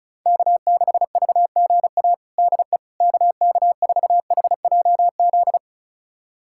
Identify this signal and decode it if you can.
Morse code, heard as a keyed tone, about 36 wpm, 700 Hz. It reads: K6VGA DE KK4HJZ
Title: USA Callsign DE Callsign 35wpm